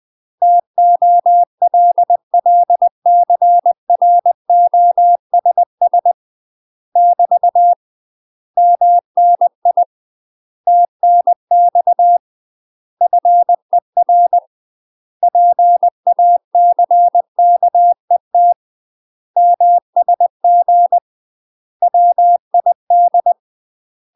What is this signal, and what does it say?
Morse code, heard as a keyed tone, about 20 wpm, 700 Hz. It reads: TOLLCROSS <BT> MNI TNX FER PACKET MSG WID
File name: UK MASTER QSOs TRAINING / 20WPM / UK MASTER QSOs 0049